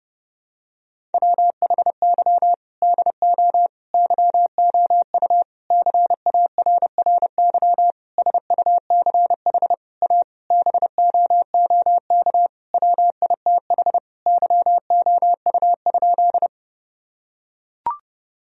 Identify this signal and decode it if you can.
Morse code, heard as a keyed tone, about 30 wpm, 700 Hz. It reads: WHY DO YOU CARRY SUCH A BOOK WITH YOU?
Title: Sentences from Top 1000 Words - Review Part 1 - 30wpm